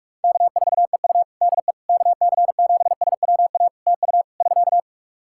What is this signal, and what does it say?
Morse code, heard as a keyed tone, about 44 wpm, 700 Hz. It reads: K4EU DE KC7SWA TU <SK>
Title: Practice QSO - Exchange 4 - Ending - 45wpm